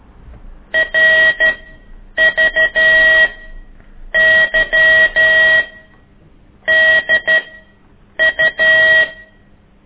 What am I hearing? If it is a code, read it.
Morse code, heard as a keyed tone, about 12 wpm, spaced as for 7 wpm, 1900 Hz. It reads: RVYDU
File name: ik1weg2 / CORSI CW